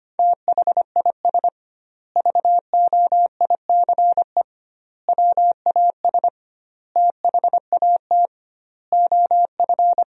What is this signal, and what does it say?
Morse code, heard as a keyed tone, about 25 wpm, 700 Hz. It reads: THIS VOICE WAS THAT OF
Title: 2x Word-Spacing - Sentences from Top 500 Words - 25wpm